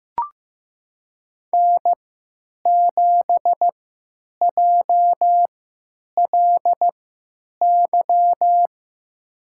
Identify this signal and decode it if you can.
Morse code, heard as a keyed tone, about 15 wpm, spaced as for 9 wpm, 700 Hz. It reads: N7JLY
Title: USA Call Signs - 5 Characters - 3x Character Spacing - 15wpm